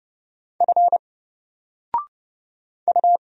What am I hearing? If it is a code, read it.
Morse code, heard as a keyed tone, about 30 wpm, 700 Hz. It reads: F U
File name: Single-Letter-Number-30wpm